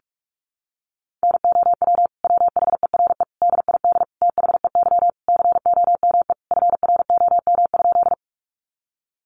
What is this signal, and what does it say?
Morse code, heard as a keyed tone, about 44 wpm, 700 Hz. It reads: NOW WHERE DID THEY COME FROM?
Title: Sentences from Top 200 Words 45wpm